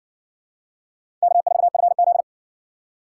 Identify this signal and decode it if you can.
Morse code, heard as a keyed tone, about 58 wpm, 700 Hz. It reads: K4FZ